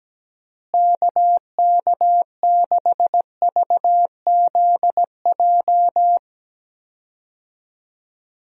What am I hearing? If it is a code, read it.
Morse code, heard as a keyed tone, about 17 wpm, 700 Hz. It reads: KK6VZJ
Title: USA Call Signs - 6 Characters - Repeated 2x - 17wpm